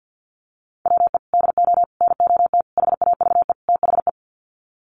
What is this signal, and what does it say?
Morse code, heard as a keyed tone, about 50 wpm, 700 Hz. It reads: WE DO NOT HAVE THE
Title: Sets of 5 Words 50wpm